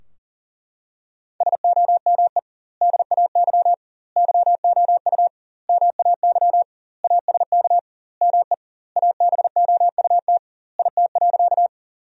Morse code, heard as a keyed tone, about 40 wpm, 700 Hz. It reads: SOME DAY YOU MAY ASK ME ABOUT IT.
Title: 2x Word-Spacing - Sentences from Top 200 Words - 40wpm